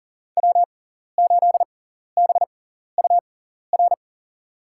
Morse code, heard as a keyed tone, about 40 wpm, spaced as for 14 wpm, 700 Hz. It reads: W8BUR
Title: USA Call Signs - 6x Character Spacing - 40wpm